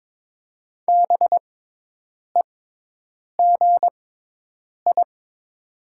Morse code, heard as a keyed tone, about 22 wpm, spaced as for 8 wpm, 700 Hz. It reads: BEGI
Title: 6x Character Spacing - Top 300 Words - 22wpm